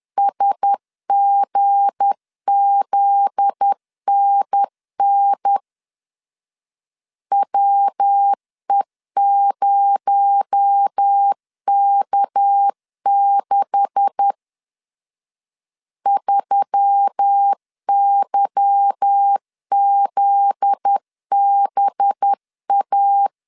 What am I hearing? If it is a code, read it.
Morse code, heard as a keyed tone, about 10 wpm, 800 Hz. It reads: SGZNN WE0K6 3YZBA